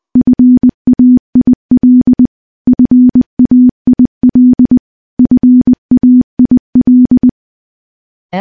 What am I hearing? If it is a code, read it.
Morse code, heard as a keyed tone, about 20 wpm, 250 Hz. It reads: FAIL FAIL FAIL
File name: morse-training-20WPM